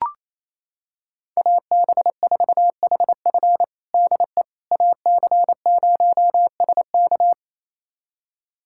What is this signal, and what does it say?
Morse code, heard as a keyed tone, about 28 wpm, 700 Hz. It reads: AB4HF DE AC0SK
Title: USA Callsign DE Callsign - 5 Characters - 28wpm